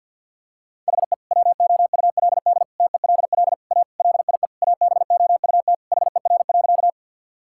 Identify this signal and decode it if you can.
Morse code, heard as a keyed tone, about 50 wpm, 700 Hz. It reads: HE WOULD TELL A LIE ABOUT HER.
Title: Sentences from Top 1600 Words 50wpm